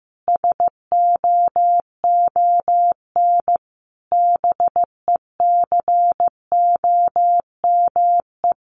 Morse code, heard as a keyed tone, about 15 wpm, 700 Hz. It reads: SOON BECOME
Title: Sentences from Top 1000 Words 15wpm